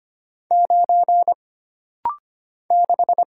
Morse code, heard as a keyed tone, about 25 wpm, 700 Hz. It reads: 9 6
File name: Single Character (Letter:Number:Punctuation:Prosign) - Rapid-Fire - 25wpm